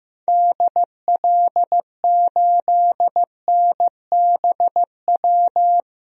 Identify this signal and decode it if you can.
Morse code, heard as a keyed tone, about 15 wpm, 700 Hz. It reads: DL8NBW
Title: DXCC-100 Call Signs - 6 Characters - 15wpm